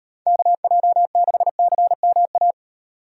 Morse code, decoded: KJ6CMA

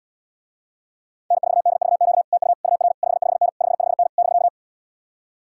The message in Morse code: N4DVQ TU UR 55N 55N <BK>

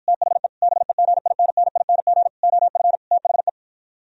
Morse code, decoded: THE BEGINNING OF THE

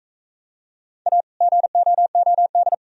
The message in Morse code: A GOOD

